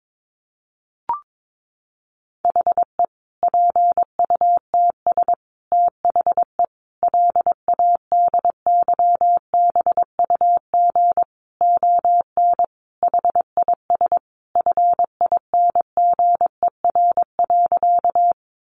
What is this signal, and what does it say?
Morse code, heard as a keyed tone, about 22 wpm, 700 Hz. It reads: HE PUTS THE LADYBUG ON HIS FINGER.
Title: Additional Easy Phrases 22wpm